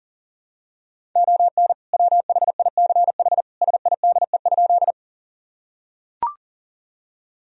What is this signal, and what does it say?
Morse code, heard as a keyed tone, about 40 wpm, 700 Hz. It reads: ON WHICH SIDE?
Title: Sentences from Top 1000 Words - Review Part 1 - 40wpm